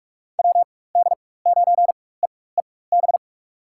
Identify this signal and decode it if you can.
Morse code, heard as a keyed tone, about 46 wpm, spaced as for 21 wpm, 700 Hz. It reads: WD9EEB
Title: USA Call Signs - 6 Characters - 4x Character Spacing - 45wpm